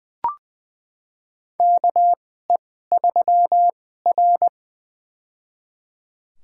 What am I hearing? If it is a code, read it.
Morse code, heard as a keyed tone, about 20 wpm, spaced as for 14 wpm, 700 Hz. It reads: KE3R
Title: USA Call Signs - 2x Character Spacing - 20wpm